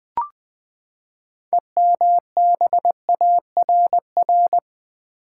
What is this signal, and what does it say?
Morse code, heard as a keyed tone, about 20 wpm, 700 Hz. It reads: EMBARR
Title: Build A Word - 6 Letters At A Time - 12-Letter Words - 20wpm